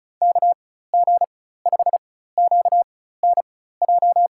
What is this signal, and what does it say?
Morse code, heard as a keyed tone, about 36 wpm, spaced as for 16 wpm, 700 Hz. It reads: KG5QNJ
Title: USA Call Signs - 4x Character Spacing - 35wpm